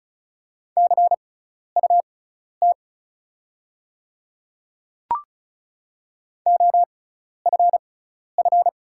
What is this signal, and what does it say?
Morse code, heard as a keyed tone, about 36 wpm, spaced as for 12 wpm, 690 Hz. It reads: CUT OFF
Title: 6x Character Spacing - Top 300 Words - 35wpm